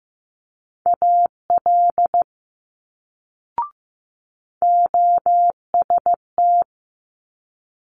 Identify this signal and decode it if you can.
Morse code, heard as a keyed tone, about 15 wpm, 700 Hz. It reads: AL OST
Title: Multi-letter Phonemes Quick Fire 15wpm